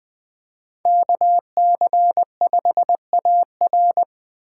Morse code, heard as a keyed tone, about 20 wpm, 700 Hz. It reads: KC5AR